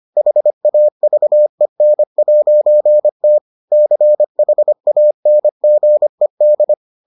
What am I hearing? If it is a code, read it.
Morse code, heard as a keyed tone, about 25 wpm, 600 Hz. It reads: HAVEN'T CHANGED